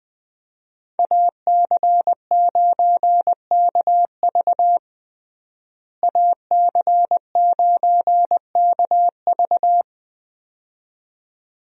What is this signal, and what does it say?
Morse code, heard as a keyed tone, about 20 wpm, 700 Hz. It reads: AC9KV AC9KV
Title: USA Call Signs - 5 Characters - Repeated 2x - 20wpm